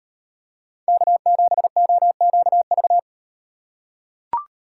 Morse code, heard as a keyed tone, about 38 wpm, 700 Hz. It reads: K7OQV